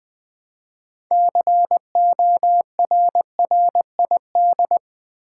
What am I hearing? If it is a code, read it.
Morse code, heard as a keyed tone, about 20 wpm, 700 Hz. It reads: CORRID
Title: Build A Word - 3 Letters At A Time - 8-Letter Words - 20wpm